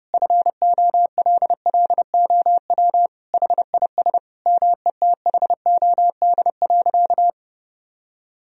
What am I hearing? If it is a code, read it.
Morse code, heard as a keyed tone, about 31 wpm, 700 Hz. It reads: FOLLOW HIS METHOD.
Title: Sentences from Top 1000 Words - Review Part 3 - 30wpm